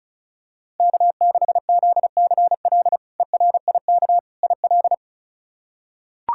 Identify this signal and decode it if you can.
Morse code, heard as a keyed tone, about 36 wpm, 700 Hz. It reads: K6ZCL ERIK IL
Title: K1USN SST QSO Practice - 35wpm